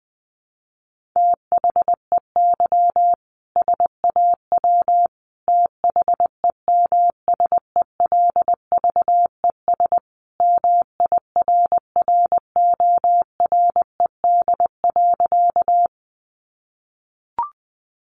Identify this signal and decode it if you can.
Morse code, heard as a keyed tone, about 19 wpm, 700 Hz. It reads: THEY SAW THEMSELVES MIRRORED.
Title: A Sound Of Thunder 20wpm